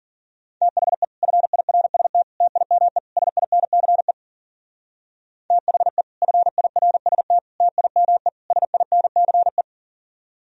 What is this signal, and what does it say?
Morse code, heard as a keyed tone, about 41 wpm, 700 Hz. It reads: THE FIRST TIME SINCE THE FIRST TIME SINCE
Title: Speed-Racing - Sets of 4 Words 40wpm